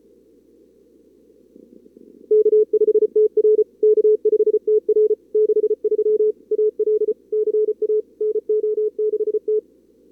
Morse code, heard as a keyed tone, about 34 wpm, 440 Hz. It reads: K5TR K5TR 63 AL CA NO6T
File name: CWSprint / Feb2022 / k5tr-feb-na-sprint-cw-k5ot -2300